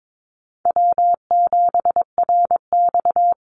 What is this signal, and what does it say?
Morse code, heard as a keyed tone, about 22 wpm, 700 Hz. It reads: W7RX
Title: Call Signs Repeated 2x - 22wpm